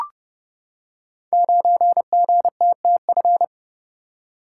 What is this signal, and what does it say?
Morse code, heard as a keyed tone, about 30 wpm, 700 Hz. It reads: 9GTTF
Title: Code Groups - 5 Letter:Numbers - 30wpm